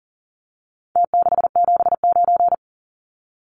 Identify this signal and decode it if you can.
Morse code, heard as a keyed tone, about 40 wpm, 700 Hz. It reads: T679
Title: Numbers - Contest Style - 4-Digit 40wpm